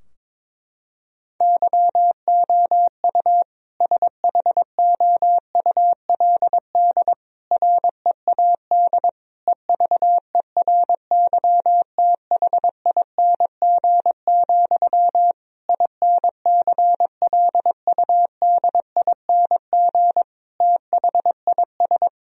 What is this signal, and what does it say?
Morse code, heard as a keyed tone, about 22 wpm, 700 Hz. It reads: YOU SHOULD READ EVERYTHING, INCLUDING THIS
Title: Sentences from Top 1000 Words - Review Part 4 - 22wpm